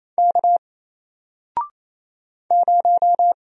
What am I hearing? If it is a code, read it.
Morse code, heard as a keyed tone, about 28 wpm, 700 Hz. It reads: K 0
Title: Single Letter-Number Alt 28wpm